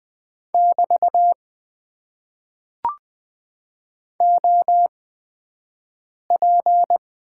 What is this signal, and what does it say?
Morse code, heard as a keyed tone, about 20 wpm, 700 Hz. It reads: <BT> O P
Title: Top 90 QSO Elements - 8x Character Spacing - 20wpm